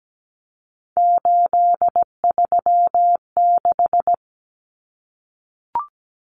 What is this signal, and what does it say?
Morse code, heard as a keyed tone, about 17 wpm, 700 Hz. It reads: 836
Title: Numbers - 3-Digit 17wpm